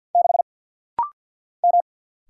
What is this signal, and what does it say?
Morse code, heard as a keyed tone, about 50 wpm, 700 Hz. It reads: 6 M
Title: Single Letter-Number - Rapid-Fire 50wpm